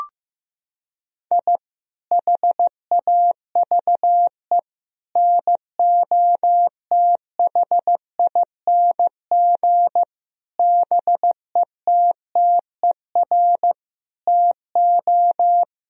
Morse code, heard as a keyed tone, about 15 wpm, 700 Hz. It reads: I HAVE NOTHING BETTER TO
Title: Sentences from Top 1000 Words - Review Part 4 - 15wpm